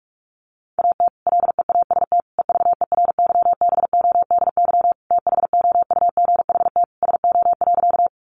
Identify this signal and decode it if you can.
Morse code, heard as a keyed tone, about 44 wpm, 700 Hz. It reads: AT LEAST EVERYBODY THOUGHT SO.
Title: Sentences from Top 1400 Words 45wpm